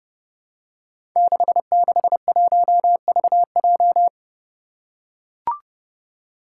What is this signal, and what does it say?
Morse code, heard as a keyed tone, about 30 wpm, 700 Hz. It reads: 661VJ
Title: Five Challenging Letter-Number Combinations 30wpm